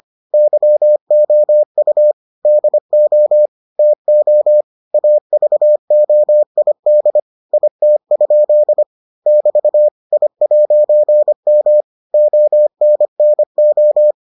Morse code, heard as a keyed tone, about 25 wpm, 600 Hz. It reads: YOU DO TO AVOID IT? <BT> I'M ONNO